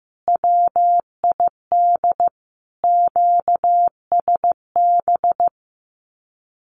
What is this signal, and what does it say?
Morse code, heard as a keyed tone, about 15 wpm, 700 Hz. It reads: WID QSB
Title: Common QSO Phrases Encore 15wpm